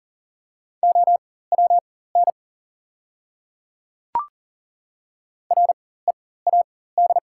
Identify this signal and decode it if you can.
Morse code, heard as a keyed tone, about 40 wpm, spaced as for 19 wpm, 700 Hz. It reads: OWN READ
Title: 4x Character Spacing - Top 300 Words - 40wpm